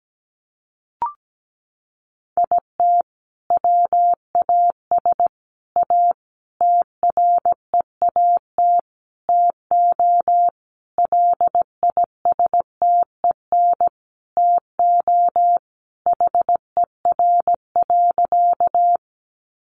Morse code, heard as a keyed tone, about 17 wpm, 700 Hz. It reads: IT WAS A TREAT TO LISTEN TO HER.
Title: Sentences from Top 1300 Words 17wpm